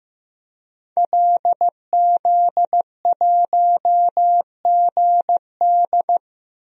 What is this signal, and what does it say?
Morse code, heard as a keyed tone, about 16 wpm, 700 Hz. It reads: LZ1GD
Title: DXCC-100 Call Signs - 5 Characters - 15wpm